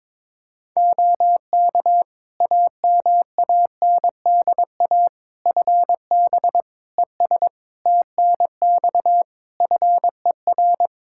Morse code, heard as a keyed tone, about 22 wpm, 700 Hz. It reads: OK AMANDA FB ES TNX FER
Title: K1USN SST QSO Practice 22wpm